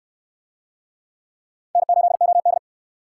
Morse code, heard as a keyed tone, about 68 wpm, 700 Hz. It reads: N8OD